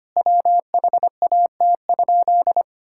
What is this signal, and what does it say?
Morse code, heard as a keyed tone, about 25 wpm, 700 Hz. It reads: WHAT?
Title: Sentences from Top 1600 Words 25wpm